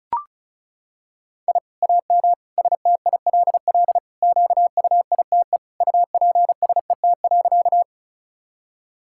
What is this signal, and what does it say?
Morse code, heard as a keyed tone, about 34 wpm, 700 Hz. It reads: I AM STILL QUITE UPSET.
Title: Sentences from Top 1000 Words - Review Part 5 - 35wpm